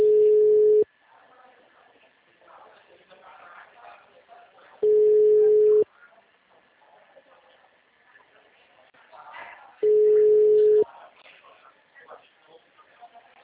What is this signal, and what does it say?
Ringback tone (Europe, 425 Hz), with speech.